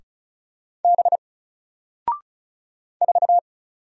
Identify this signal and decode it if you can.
Morse code, heard as a keyed tone, about 36 wpm, 700 Hz. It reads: B 4
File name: Single-Letter-Number-35wpm